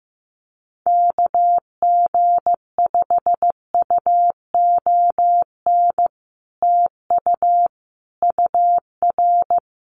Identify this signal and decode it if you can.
Morse code, heard as a keyed tone, about 15 wpm, 700 Hz. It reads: KG5UON TU UR
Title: Common QSO Phrases Encore 15wpm